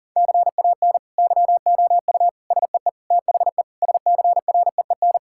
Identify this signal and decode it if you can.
Morse code, heard as a keyed tone, about 40 wpm, 700 Hz. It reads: CAN YOU SEE THE SCREEN
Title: Sentences from Top 1000 Words - Review Part 6 - 40wpm